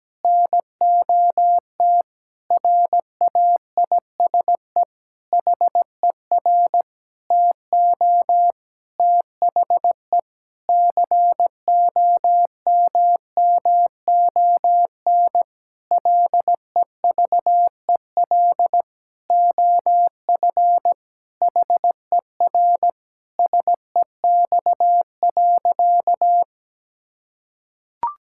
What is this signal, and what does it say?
Morse code, heard as a keyed tone, about 17 wpm, 700 Hz. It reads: NOT RAISE HER TO THE COMMON LEVEL OF HER SEX.
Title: Sentences from Top 1600 Words 17wpm